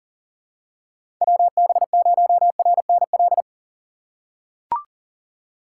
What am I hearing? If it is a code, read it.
Morse code, heard as a keyed tone, about 40 wpm, 700 Hz. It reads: WB0RNL
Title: USA Call Signs - 3x Character Spacing - 40wpm